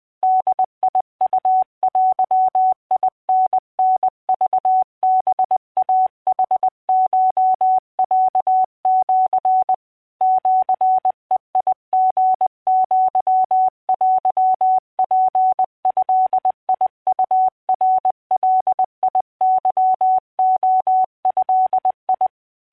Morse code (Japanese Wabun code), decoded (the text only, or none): ボウデタタクハイヌコロシ シベリアテツドウナガケレド